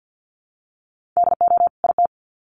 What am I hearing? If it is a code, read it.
Morse code, heard as a keyed tone, about 50 wpm, 700 Hz. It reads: DO IT
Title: Sets of 2 Words 50wpm